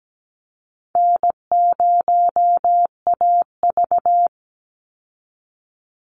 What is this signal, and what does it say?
Morse code, heard as a keyed tone, about 17 wpm, 700 Hz. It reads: N0AV
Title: Easy Call Signs Repeated 2x - 17wpm